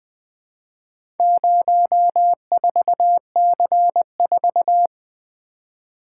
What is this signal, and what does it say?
Morse code, heard as a keyed tone, about 20 wpm, 700 Hz. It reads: 04C4
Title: Four Letter-Number Combinations - 20wpm